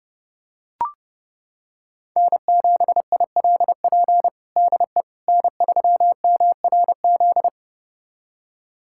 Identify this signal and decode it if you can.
Morse code, heard as a keyed tone, about 30 wpm, 700 Hz. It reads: N7ILP DE N3MRZ
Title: USA Callsign DE Callsign - 5 Characters - 30wpm